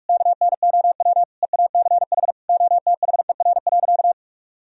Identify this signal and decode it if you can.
Morse code, heard as a keyed tone, about 42 wpm, 700 Hz. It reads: KNOW EACH OTHER.
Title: Sentences from Top 1000 Words - Review Part 3 - 45wpm